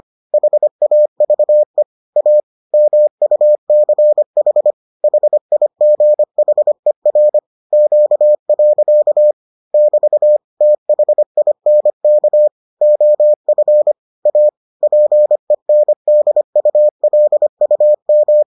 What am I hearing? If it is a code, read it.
Morse code, heard as a keyed tone, about 25 wpm, 600 Hz. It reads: HAVE A MUCH HIGHER Q. <BT> THINK OF A PENDULUM